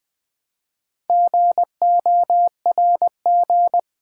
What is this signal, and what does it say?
Morse code, heard as a keyed tone, about 20 wpm, 700 Hz. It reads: GORG
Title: Build A Word - 2 Letters At A Time - 8-Letter Words - 20wpm